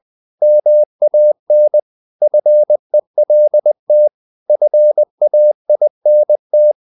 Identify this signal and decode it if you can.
Morse code, heard as a keyed tone, about 20 wpm, 600 Hz. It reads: MAN FELT FAINT
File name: the old man and the sea part 2 20wpm